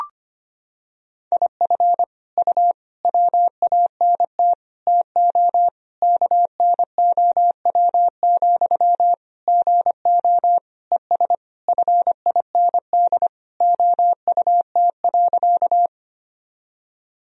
Morse code, heard as a keyed tone, about 25 wpm, 700 Hz. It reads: IF U WANT TO KNOW, GO ES FIND OUT.
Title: Sentences from Top 200 Words and Abbreviations 25wpm